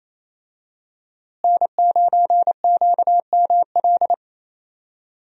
Morse code, heard as a keyed tone, about 28 wpm, 700 Hz. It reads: N9QML
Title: USA Call Signs - 5 Characters - Repeated 2x - 28wpm